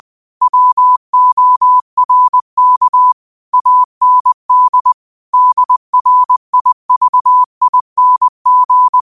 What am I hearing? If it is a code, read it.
Morse code, heard as a keyed tone, about 20 wpm, 1000 Hz. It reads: WORK AND DRIVING
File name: 20wpm